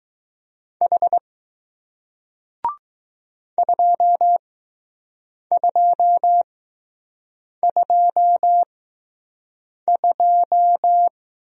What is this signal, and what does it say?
Morse code, heard as a keyed tone, about 20 wpm, 700 Hz. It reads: H 2 2 2 2